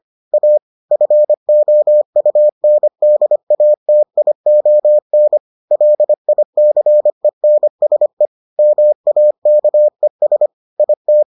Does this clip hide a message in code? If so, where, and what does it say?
Yes: Morse code from 0.3 to 11.2 s, heard as a keyed tone, about 25 wpm, 600 Hz. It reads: A FOUNDATION LICENSE MAKES IT